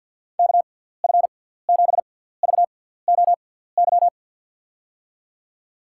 Morse code, heard as a keyed tone, about 53 wpm, spaced as for 17 wpm, 700 Hz. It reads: KF7VOY
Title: USA Call Signs - 6 Characters - 6x Character Spacing - 50wpm